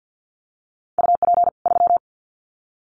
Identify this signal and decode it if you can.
Morse code, heard as a keyed tone, about 48 wpm, 700 Hz. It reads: UP 3
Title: Common QSO Phrases 50wpm